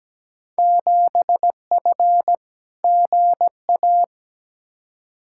Morse code, heard as a keyed tone, about 17 wpm, 700 Hz. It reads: 7F GA
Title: Field Day Exchanges Encore - 17wpm